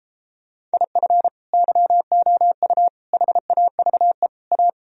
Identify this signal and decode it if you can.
Morse code, heard as a keyed tone, about 32 wpm, 700 Hz. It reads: IF YOU HAVE A